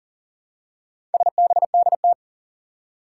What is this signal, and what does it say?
Morse code, heard as a keyed tone, about 41 wpm, 700 Hz. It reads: SBDT
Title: Four Challenging Letter-Number Combinations 40wpm